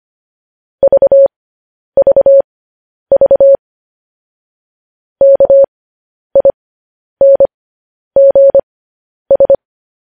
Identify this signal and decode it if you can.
Morse code, heard as a keyed tone, about 25 wpm, spaced as for 10 wpm, 550 Hz. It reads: VVV KINGS
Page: news-headlines-in-morse-code-at-10-wpm